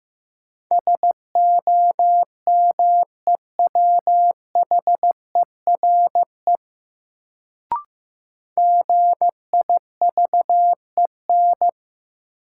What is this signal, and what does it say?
Morse code, heard as a keyed tone, about 15 wpm, 700 Hz. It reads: SOMEWHERE GIVEN
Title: Sentences from Top 1400 Words - New Word Drill - 15wpm